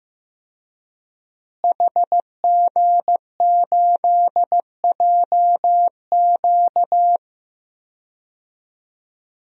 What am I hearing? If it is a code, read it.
Morse code, heard as a keyed tone, about 15 wpm, 700 Hz. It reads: HG8JQ
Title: DXCC-100 Call Signs - 5 Characters - Repeated 3x - 15wpm